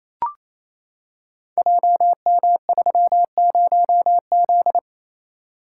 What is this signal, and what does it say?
Morse code, heard as a keyed tone, about 28 wpm, 700 Hz. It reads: JM30Z
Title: Code Groups - 5 Letter:Numbers - 28wpm